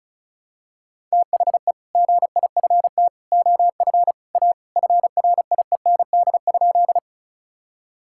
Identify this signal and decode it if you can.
Morse code, heard as a keyed tone, about 36 wpm, 700 Hz. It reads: THE GIFT OF A FRIEND?
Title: Sentences from Top 1000 Words - Review Part 6 - 35wpm